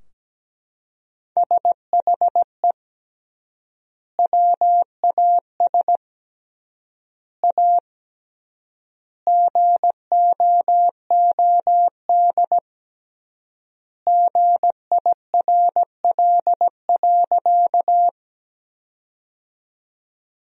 Morse code, heard as a keyed tone, about 17 wpm, 700 Hz. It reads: SHE WAS A GOOD GIRL.
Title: 3x Word-Spacing - Sentences from Top 400 Words - 17wpm